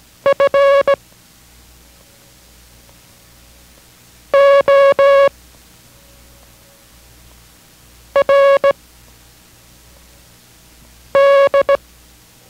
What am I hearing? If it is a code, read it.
Morse code, heard as a keyed tone, about 18 wpm, spaced as for 3 wpm, 560 Hz. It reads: FORD